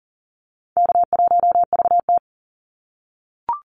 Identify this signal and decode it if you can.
Morse code, heard as a keyed tone, about 40 wpm, 700 Hz. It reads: K1VT
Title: Call Signs - 40wpm